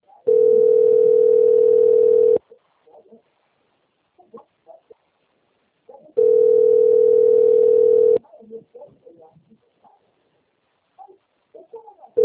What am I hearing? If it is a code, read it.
Ringback tone (North America), 440 + 480 Hz.